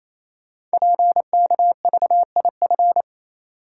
Morse code, heard as a keyed tone, about 27 wpm, 700 Hz. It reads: PKVIF
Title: Code Groups - 5 Letters - 28wpm